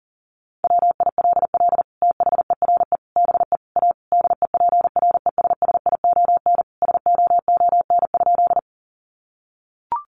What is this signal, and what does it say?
Morse code, heard as a keyed tone, about 40 wpm, 700 Hz. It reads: WILL THERE BE A DEPRESSION SOON?
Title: Sentences from Top 1300 Words 40wpm